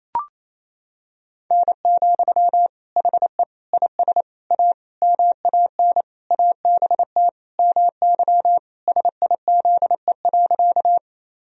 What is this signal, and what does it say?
Morse code, heard as a keyed tone, about 27 wpm, 700 Hz. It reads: N, HE IS A MAN ABT MY SIZE.
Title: Sentences from Top 400 Words and Abbreviations 28wpm